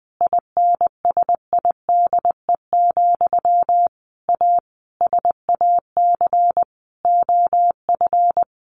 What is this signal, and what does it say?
Morse code, heard as a keyed tone, about 20 wpm, 700 Hz. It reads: INSIDE, A SAC OF
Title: A Sound Of Thunder 20wpm